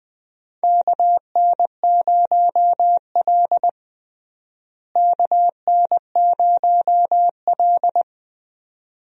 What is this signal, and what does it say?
Morse code, heard as a keyed tone, about 20 wpm, 700 Hz. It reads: KN0L KN0L
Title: USA Call Signs - 4 Characters - Repeated 2x - 20wpm